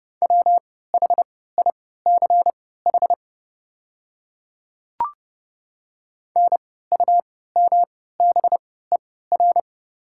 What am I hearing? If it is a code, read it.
Morse code, heard as a keyed tone, about 30 wpm, spaced as for 17 wpm, 700 Hz. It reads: WHICH NUMBER
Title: 3x Character Spacing - Top 100 Words - 30wpm